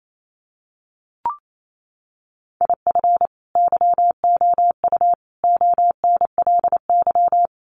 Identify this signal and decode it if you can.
Morse code, heard as a keyed tone, about 28 wpm, 700 Hz. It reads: IF YOU ONLY
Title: Sentences from Top 400 Words 28wpm